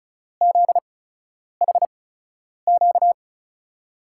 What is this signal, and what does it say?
Morse code, heard as a keyed tone, about 34 wpm, spaced as for 10 wpm, 700 Hz. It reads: ZHQ
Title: USA Call Signs - 5 Characters - 8x Character Spacing - 35wpm